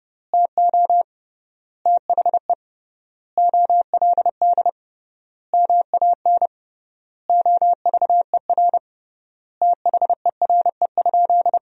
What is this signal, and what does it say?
Morse code, heard as a keyed tone, about 32 wpm, 700 Hz. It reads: TO THE OLD MAN OVER THERE?
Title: 3x Word-Spacing - Sentences from Top 200 Words - 30wpm